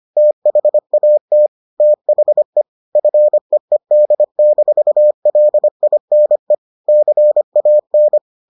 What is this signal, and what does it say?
Morse code, heard as a keyed tone, about 25 wpm, 600 Hz. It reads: THAT THE FEED-LINE CAN